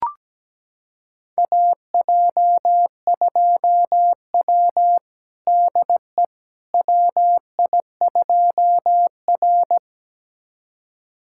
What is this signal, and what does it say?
Morse code, heard as a keyed tone, about 17 wpm, 700 Hz. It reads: AJ2W DE WI2R
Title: USA Callsign DE Callsign - 4 Characters - 17wpm